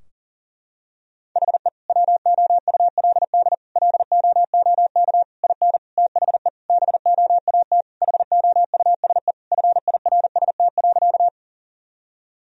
Morse code, heard as a keyed tone, about 37 wpm, 700 Hz. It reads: HE WOULD LOOK IN THE BOAT HOUSE FIRST.
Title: Sentences from Top 1000 Words - Review Part 3 - 40wpm